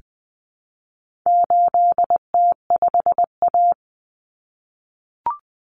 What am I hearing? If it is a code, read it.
Morse code, heard as a keyed tone, about 20 wpm, 700 Hz. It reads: 8T5A
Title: Common Cut Numbers - 4-Digit 20wpm